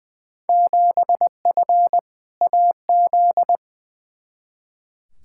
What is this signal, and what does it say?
Morse code, heard as a keyed tone, about 20 wpm, 700 Hz. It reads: 7F AZ
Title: Field Day Exchanges Encore - 20wpm